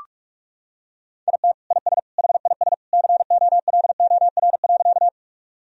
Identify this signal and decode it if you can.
Morse code, heard as a keyed tone, about 42 wpm, 700 Hz. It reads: IT IS HIS COLOR.
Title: Sentences from Top 1000 Words - Review Part 2 - 45wpm